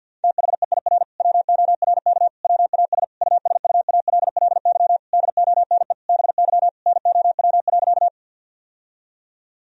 Morse code, heard as a keyed tone, about 50 wpm, 700 Hz. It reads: THEIR WORK WAS USUALLY DONE BY NOW.
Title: Sentences from Top 1000 Words - Review Part 3 - 50wpm